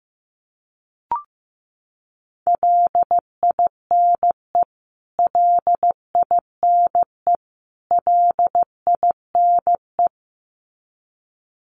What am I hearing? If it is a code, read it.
Morse code, heard as a keyed tone, about 15 wpm, 700 Hz. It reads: LINE LINE LINE
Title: Top 300 Words Repeated 15wpm